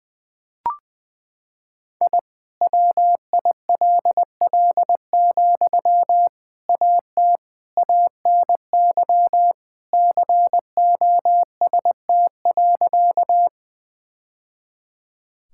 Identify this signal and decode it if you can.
Morse code, heard as a keyed tone, about 20 wpm, 700 Hz. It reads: I WILL, AT ANY COST.
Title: Sentences from Top 1000 Words - Review Part 5 - 20wpm